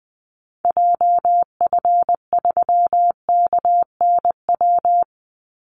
Morse code, heard as a keyed tone, about 20 wpm, 700 Hz. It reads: JF3KNW
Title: Call Signs Repeated 3x - 20wpm